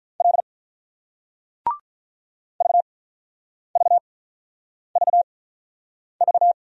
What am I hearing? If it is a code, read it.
Morse code, heard as a keyed tone, about 46 wpm, spaced as for 9 wpm, 690 Hz. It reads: L VVVV